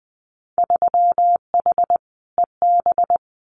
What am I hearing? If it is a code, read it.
Morse code, heard as a keyed tone, about 20 wpm, 700 Hz. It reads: 3H EB
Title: Winter Field Day Exchanges Encore 20wpm